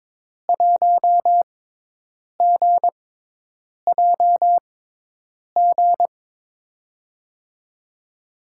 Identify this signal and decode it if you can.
Morse code, heard as a keyed tone, about 22 wpm, spaced as for 8 wpm, 700 Hz. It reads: 1GJG
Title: USA Call Signs - 6x Character Spacing - 22wpm